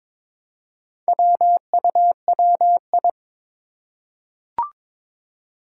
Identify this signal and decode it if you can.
Morse code, heard as a keyed tone, about 22 wpm, 700 Hz. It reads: WUWI
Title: Four Challenging Letter-Number Combinations 22wpm